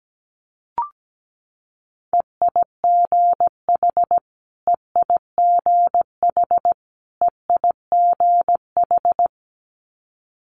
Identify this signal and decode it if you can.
Morse code, heard as a keyed tone, about 17 wpm, 700 Hz. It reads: EIGH EIGH EIGH
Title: Multi-letter Phonemes Repeated 3x 17wpm